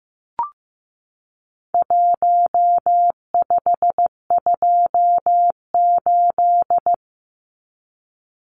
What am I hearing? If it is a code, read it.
Morse code, heard as a keyed tone, about 15 wpm, 700 Hz. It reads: 1528
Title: Numbers - Contest Style - 4-Digit 15wpm